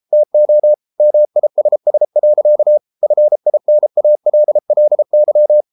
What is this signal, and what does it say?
Morse code, heard as a keyed tone, about 33 wpm, 600 Hz. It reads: TO MISS. FINALLY